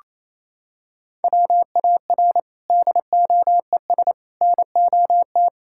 Morse code, heard as a keyed tone, about 28 wpm, 700 Hz. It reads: WAR DOES NOT
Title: Sentences from Top 1000 Words - Review Part 5 - 28wpm